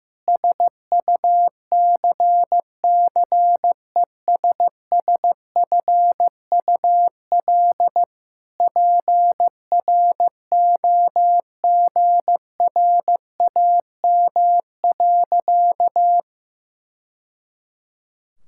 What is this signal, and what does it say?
Morse code, heard as a keyed tone, about 15 wpm, 700 Hz. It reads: SUCCESSFUL PROGRAM.
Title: Sentences from Top 1000 Words - Review Part 5 - 15wpm